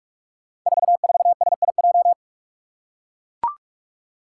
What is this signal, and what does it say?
Morse code, heard as a keyed tone, about 44 wpm, 700 Hz. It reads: 44SIJ
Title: Five Challenging Letter-Number Combinations 45wpm